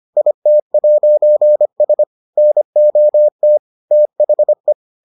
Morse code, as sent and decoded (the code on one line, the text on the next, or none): .. - .----. ... / -. --- - / - .... .
IT'S NOT THE